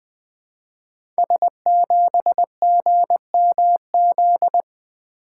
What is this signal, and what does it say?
Morse code, heard as a keyed tone, about 20 wpm, 700 Hz. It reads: S7GMZ
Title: Code Groups - 5 Letter:Numbers - 20wpm